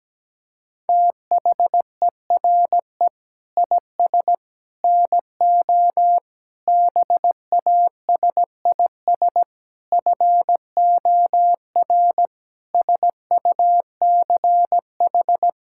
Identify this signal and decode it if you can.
Morse code, heard as a keyed tone, about 17 wpm, 700 Hz. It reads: THERE IS NO BASIS FOR SUCH
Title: Sentences from Top 1000 Words - Review Part 5 - 17wpm